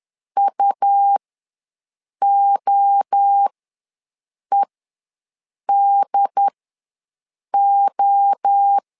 Morse code, heard as a keyed tone, about 10 wpm, spaced as for 6 wpm, 800 Hz. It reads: UOEDO
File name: LEZIONI / 3 MWRO / 3 MWRO ws 10wpm 2310